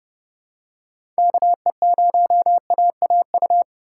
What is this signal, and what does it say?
Morse code, heard as a keyed tone, about 31 wpm, 700 Hz. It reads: KE0AAU
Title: USA Call Signs - 6 Characters - 2x Character Spacing - 30wpm